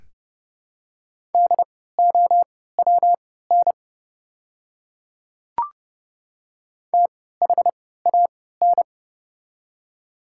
Morse code, heard as a keyed tone, about 30 wpm, spaced as for 17 wpm, 700 Hz. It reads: DOWN THAN
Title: 3x Character Spacing - Top 100 Words - 30wpm